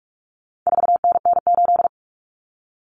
Morse code, heard as a keyed tone, about 44 wpm, 700 Hz. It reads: 4NN8
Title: Common Cut Numbers - 4-Digit 45wpm